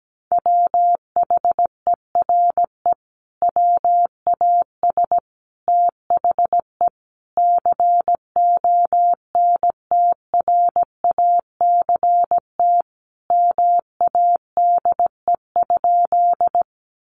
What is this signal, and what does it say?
Morse code, heard as a keyed tone, about 17 wpm, 700 Hz. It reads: WHERE WAS THE CONTRACT MADE?
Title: Sentences from Top 1400 Words 17wpm